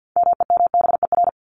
Morse code, heard as a keyed tone, about 50 wpm, 700 Hz. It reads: MEMBER
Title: Top 1001-1250 Words 50wpm